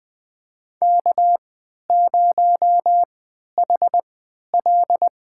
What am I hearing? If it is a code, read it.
Morse code, heard as a keyed tone, about 20 wpm, spaced as for 11 wpm, 700 Hz. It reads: K0HL
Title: USA Call Signs - 4 Characters - 3x Character Spacing - 20wpm